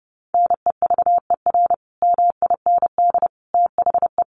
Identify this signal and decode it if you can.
Morse code, heard as a keyed tone, about 30 wpm, 700 Hz. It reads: NEVER MIND THE
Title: Sentences from Top 1400 Words 30wpm